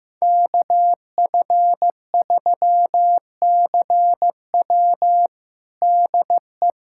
Morse code, heard as a keyed tone, about 16 wpm, 700 Hz. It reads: KF3CW DE
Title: USA Callsign DE Callsign - 5 Characters - 15wpm